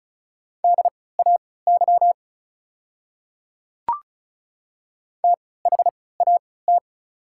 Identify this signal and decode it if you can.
Morse code, heard as a keyed tone, about 36 wpm, spaced as for 20 wpm, 700 Hz. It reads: DAY THAT
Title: 3x Character Spacing - Top 100 Words - 35wpm